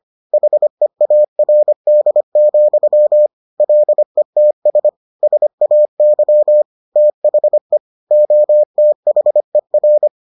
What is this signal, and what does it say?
Morse code, heard as a keyed tone, about 25 wpm, 600 Hz. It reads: HEARD, LETS SAY THE OTHER